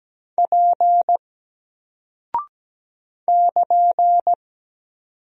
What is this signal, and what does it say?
Morse code, heard as a keyed tone, about 17 wpm, 700 Hz. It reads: P <KN>
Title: Single Character (Letter:Number:Punctuation:Prosign) Alt 17wpm